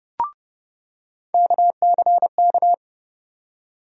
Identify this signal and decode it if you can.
Morse code, heard as a keyed tone, about 31 wpm, 700 Hz. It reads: KCK
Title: USA Call Sign Postfixes - 3 Letters 30wpm